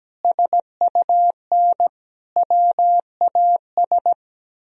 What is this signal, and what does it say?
Morse code, heard as a keyed tone, about 17 wpm, 700 Hz. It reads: SUN WAS
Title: Sentences from Top 1000 Words - Review Part 2 - 17wpm